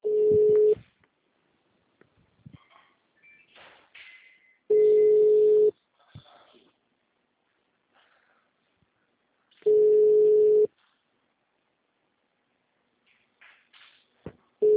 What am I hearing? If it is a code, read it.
Ringback tone (Europe, 425 Hz).
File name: out- -2020- - - .8627